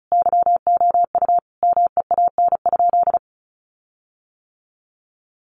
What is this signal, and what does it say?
Morse code, heard as a keyed tone, about 34 wpm, 700 Hz. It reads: YOU MEAN?
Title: Sentences from Top 1300 Words 35wpm